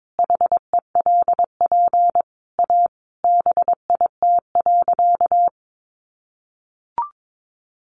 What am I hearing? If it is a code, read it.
Morse code, heard as a keyed tone, about 22 wpm, 700 Hz. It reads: HELP A BIT.
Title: Sentences from Top 900 Words 22wpm